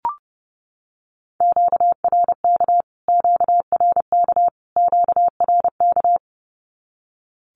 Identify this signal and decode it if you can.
Morse code, heard as a keyed tone, about 30 wpm, 700 Hz. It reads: QRK QRK QRK
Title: Q-Codes Repeated 3x 30wpm